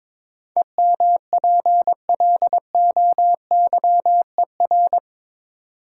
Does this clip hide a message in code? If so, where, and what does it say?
Yes: Morse code from 0.6 to 5.0 s, heard as a keyed tone, about 22 wpm, 700 Hz. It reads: EMPLOYER